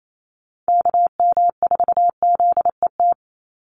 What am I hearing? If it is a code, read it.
Morse code, heard as a keyed tone, about 28 wpm, 700 Hz. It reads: KM4ZET
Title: USA Call Signs 28wpm